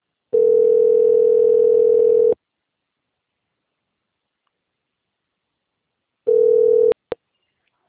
Ringback tone (North America), 440 + 480 Hz.
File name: out- - - - - .75571